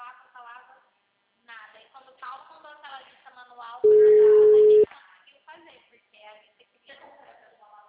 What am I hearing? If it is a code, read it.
Ringback tone (Europe, 425 Hz), with speech.